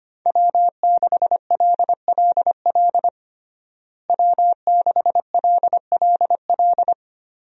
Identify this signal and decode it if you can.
Morse code, heard as a keyed tone, about 25 wpm, 700 Hz. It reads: W6LLL W6LLL
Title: USA Call Signs - 5 Characters - Repeated 3x - 25wpm